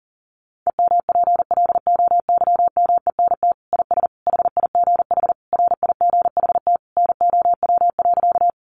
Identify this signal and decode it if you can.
Morse code, heard as a keyed tone, about 40 wpm, 700 Hz. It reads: EMPLOYMENT IS HIGH RIGHT NOW.
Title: Sentences from Top 1500 Words 40wpm